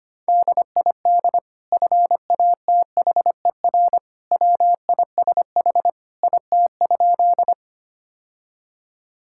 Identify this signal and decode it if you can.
Morse code, heard as a keyed tone, about 25 wpm, 700 Hz. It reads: DID FATHER WISH IT?
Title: Sentences from Top 1600 Words 25wpm